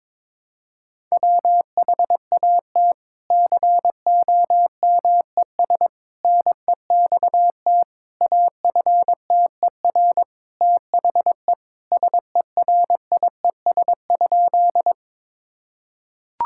Morse code, heard as a keyed tone, about 22 wpm, 700 Hz. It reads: WHAT COMES NEXT AFTER THE SERIES?
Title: Sentences from Top 1000 Words - Review Part 5 - 22wpm